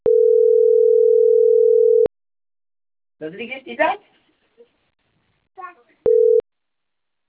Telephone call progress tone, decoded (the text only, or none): ringback tone